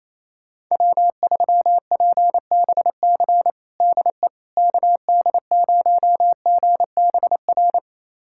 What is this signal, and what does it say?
Morse code, heard as a keyed tone, about 28 wpm, 700 Hz. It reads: W3PBC DE KD0GBR
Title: USA Callsign DE Callsign 28wpm